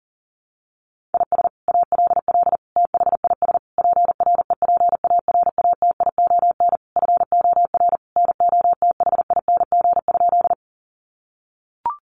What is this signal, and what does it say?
Morse code, heard as a keyed tone, about 37 wpm, 700 Hz. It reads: IS ALL THIS PREPARATION FOR NOTHING?
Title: Sentences from Top 1400 Words 40wpm